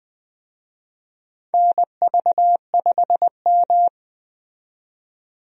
Morse code, heard as a keyed tone, about 20 wpm, 700 Hz. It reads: NV5M
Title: USA Call Signs - 4 Characters - 20wpm